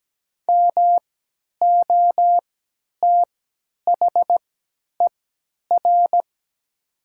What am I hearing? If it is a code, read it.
Morse code, heard as a keyed tone, about 17 wpm, spaced as for 10 wpm, 700 Hz. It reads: MOTHER
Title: 3x Character Spacing - Top 200 Words - 17wpm